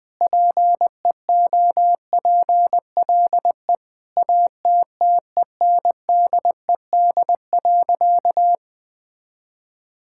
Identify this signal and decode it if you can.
Morse code, heard as a keyed tone, about 21 wpm, 700 Hz. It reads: PEOPLE ATTENDED.
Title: Sentences with 4-Word Phrases - Spoken Beginning - 20wpm